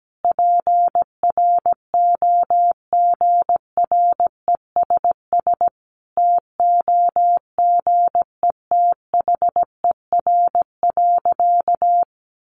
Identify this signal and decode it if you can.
Morse code, heard as a keyed tone, about 17 wpm, 700 Hz. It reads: PROGRESS TOGETHER.
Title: Sentences from Top 1400 Words 17wpm